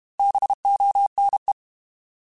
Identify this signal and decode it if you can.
Morse code, heard as a keyed tone, about 32 wpm, 800 Hz. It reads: BONE
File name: Alphabetical / Alphabetical mp3s / Bs @30